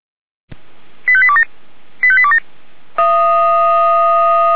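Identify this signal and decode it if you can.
Five-tone selective call (ZVEI), heard as quick runs of five tones, tones of about 70 ms. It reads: 85827 85827
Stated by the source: ZVEI five-tone selective call received on an SDR.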